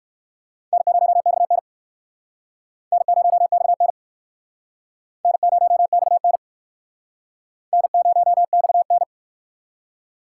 Morse code, heard as a keyed tone, about 62 wpm, 700 Hz. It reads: N0XN N0XN N0XN N0XN